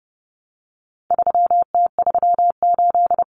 Morse code, heard as a keyed tone, about 30 wpm, 700 Hz. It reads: 3T38
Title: Common Cut Numbers - 4-Digit 30wpm